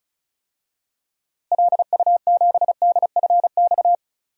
Morse code, heard as a keyed tone, about 36 wpm, 700 Hz. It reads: LU7DFX
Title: DXCC-100 Call Signs - 6 Characters - Repeated 2x - 35wpm